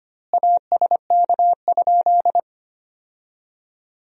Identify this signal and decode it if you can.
Morse code, heard as a keyed tone, about 25 wpm, 700 Hz. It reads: ASK?
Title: 2x Word-Spacing - Sentences from Top 200 Words - 25wpm